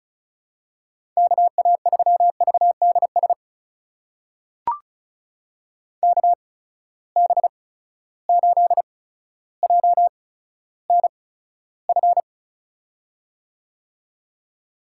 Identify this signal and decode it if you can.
Morse code, heard as a keyed tone, about 35 wpm, spaced as for 15 wpm, 700 Hz. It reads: KA3VDS KB8JNF
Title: USA Call Signs - 6 Characters - 8x Character Spacing - 35wpm